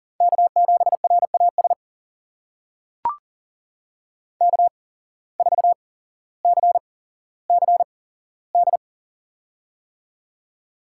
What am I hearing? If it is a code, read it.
Morse code, heard as a keyed tone, about 38 wpm, spaced as for 17 wpm, 700 Hz. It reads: K7RAS K4CCD
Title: USA Call Signs - 5 Characters - 8x Character Spacing - 40wpm